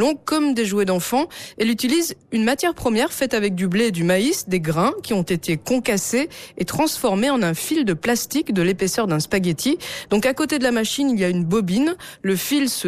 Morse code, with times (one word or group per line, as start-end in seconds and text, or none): none